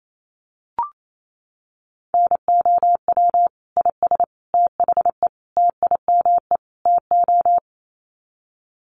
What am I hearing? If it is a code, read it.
Morse code, heard as a keyed tone, about 28 wpm, 700 Hz. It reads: NOW IS THE TIME TO
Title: Sets of 5 Words Encore2 28wpm